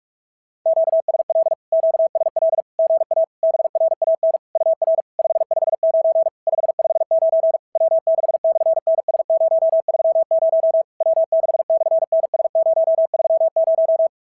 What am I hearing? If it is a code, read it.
Morse code, heard as a keyed tone, about 46 wpm, 690 Hz. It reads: QSL QSL GA BRAN UR 559 559 W6/NS030 W6/NS030
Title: SOTA QSO Practice 45wpm